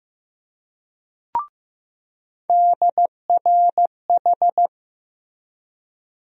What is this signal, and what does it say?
Morse code, heard as a keyed tone, about 15 wpm, 700 Hz. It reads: DRH
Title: USA Call Sign Postfixes - 3 Letters 15wpm